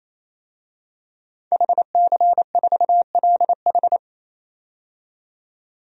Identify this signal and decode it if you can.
Morse code, heard as a keyed tone, about 28 wpm, 700 Hz. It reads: HC4LH
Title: DXCC-100 Call Signs - 5 Characters - 28wpm